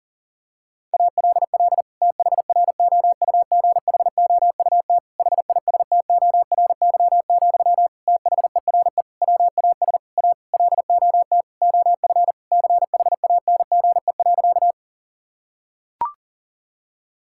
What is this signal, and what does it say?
Morse code, heard as a keyed tone, about 40 wpm, 700 Hz. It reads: ALL THROUGHOUT HISTORY, THERE WAS A LOT OF CHANGE.
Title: Sentences from Top 1000 Words - Review Part 5 - 40wpm